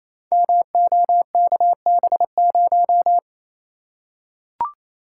Morse code, heard as a keyed tone, about 28 wpm, 700 Hz. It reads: MOKB0
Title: Code Groups - 5 Letter:Numbers - 28wpm